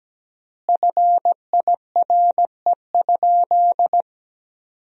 Morse code, heard as a keyed tone, about 17 wpm, 700 Hz. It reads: FIRE?
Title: 3x Word-Spacing - Sentences from Top 400 Words - 17wpm